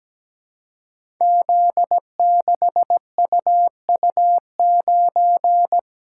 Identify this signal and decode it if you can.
Morse code, heard as a keyed tone, about 17 wpm, 700 Hz. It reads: Z6UU9
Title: Code Groups - 5 Letter:Numbers - 17wpm